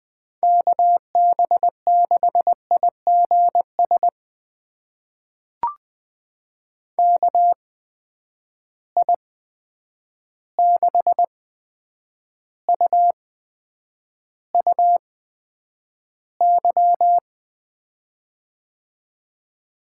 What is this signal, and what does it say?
Morse code, heard as a keyed tone, about 20 wpm, spaced as for 9 wpm, 700 Hz. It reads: KB6IGS KI6UUY
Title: USA Call Signs - 6 Characters - 8x Character Spacing - 20wpm